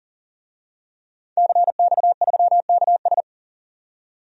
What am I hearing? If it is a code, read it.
Morse code, heard as a keyed tone, about 40 wpm, 700 Hz. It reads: CX3KS
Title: DXCC-100 Call Signs - 5 Characters - 40wpm